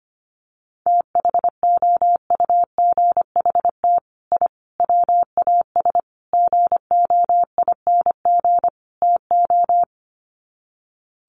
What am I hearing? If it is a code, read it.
Morse code, heard as a keyed tone, about 25 wpm, 700 Hz. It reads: THOUGHT I WAS GOING TO
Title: Sets of 5 Words Encore2 25wpm